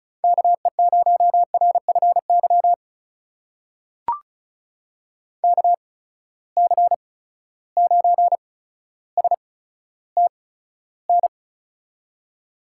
Morse code, heard as a keyed tone, about 34 wpm, spaced as for 15 wpm, 700 Hz. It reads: KE0RFY KC9STN
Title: USA Call Signs - 6 Characters - 8x Character Spacing - 35wpm